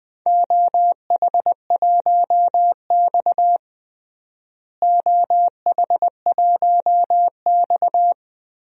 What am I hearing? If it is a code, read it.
Morse code, heard as a keyed tone, about 20 wpm, 700 Hz. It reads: OH1X OH1X
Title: DXCC-100 Call Signs - 4 Characters - Repeated 3x - 20wpm